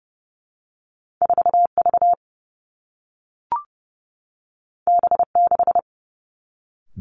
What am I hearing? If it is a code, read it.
Morse code, heard as a keyed tone, about 30 wpm, 700 Hz. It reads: 4 V B 6
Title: Two Challenging Letter-Number Combinations 30wpm